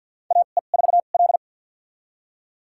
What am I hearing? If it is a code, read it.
Morse code, heard as a keyed tone, about 48 wpm, spaced as for 35 wpm, 690 Hz. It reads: AE4L
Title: USA Call Signs - 2x Character Spacing - 50wpm